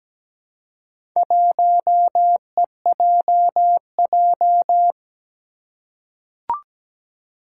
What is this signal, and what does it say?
Morse code, heard as a keyed tone, about 17 wpm, 700 Hz. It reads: 1EJJ
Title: Four Challenging Letter-Number Combinations 17wpm